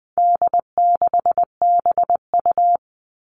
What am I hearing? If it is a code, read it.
Morse code, heard as a keyed tone, about 20 wpm, 700 Hz. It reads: D6BU
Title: All Cut Numbers - 4-Digit 20wpm